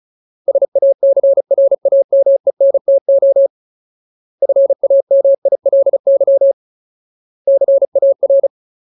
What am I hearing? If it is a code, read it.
Morse code, heard as a keyed tone, about 34 wpm, 550 Hz. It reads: SACRAMENTO FAMILY CAR
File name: Ultimate Word List / 35WPM Ultimate